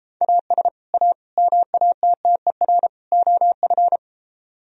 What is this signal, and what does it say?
Morse code, heard as a keyed tone, about 31 wpm, 700 Hz. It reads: AS A MATTER OF